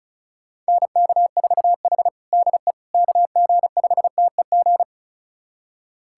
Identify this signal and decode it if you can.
Morse code, heard as a keyed tone, about 34 wpm, 700 Hz. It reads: NK4H DE KG5TEG
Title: USA Callsign DE Callsign 35wpm